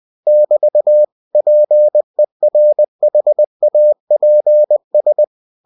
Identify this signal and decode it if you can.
Morse code, heard as a keyed tone, about 20 wpm, 600 Hz. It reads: <BT> PERHAPS
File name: cw-books / winnie-the-pooh / winnie-20wpm-page-0007